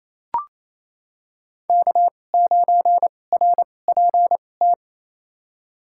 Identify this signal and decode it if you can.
Morse code, heard as a keyed tone, about 27 wpm, spaced as for 20 wpm, 700 Hz. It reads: K9RPT
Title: USA Call Signs - 2x Character Spacing - 28wpm